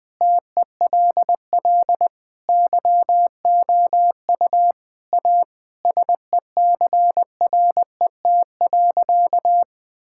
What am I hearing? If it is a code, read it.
Morse code, heard as a keyed tone, about 20 wpm, 700 Hz. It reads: TELL YOU A SECRET.
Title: Sentences with 3-Word Phrases - 20wpm